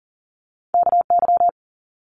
Morse code, heard as a keyed tone, about 40 wpm, 700 Hz. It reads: KY
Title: ARRL Sections 40wpm